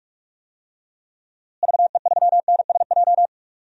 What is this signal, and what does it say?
Morse code, heard as a keyed tone, about 42 wpm, 700 Hz. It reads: VE3NSJ
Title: DXCC-100 Call Signs - 6 Characters - Repeated 3x - 45wpm